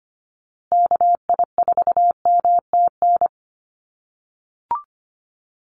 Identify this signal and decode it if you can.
Morse code, heard as a keyed tone, about 25 wpm, 700 Hz. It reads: KI4MTN
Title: USA Call Signs Repeated 2x 25wpm